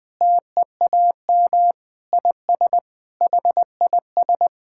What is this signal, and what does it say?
Morse code, heard as a keyed tone, about 19 wpm, 700 Hz. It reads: TEAM IS HIS
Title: Sentences with 4-Word Phrases - Spoken Beginning - 20wpm